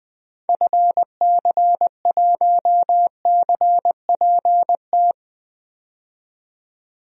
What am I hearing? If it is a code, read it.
Morse code, heard as a keyed tone, about 20 wpm, 700 Hz. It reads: FC1CPT
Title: DXCC-100 Call Signs - 6 Characters - 20wpm